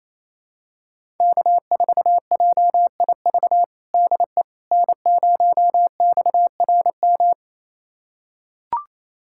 Morse code, heard as a keyed tone, about 28 wpm, 700 Hz. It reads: K4JIV DE N0XRM
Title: USA Callsign DE Callsign - 5 Characters - 28wpm